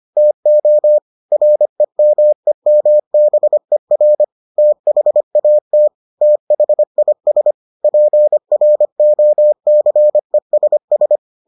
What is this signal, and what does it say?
Morse code, heard as a keyed tone, about 25 wpm, 600 Hz. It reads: TO REMEMBER THAT THIS PROCESS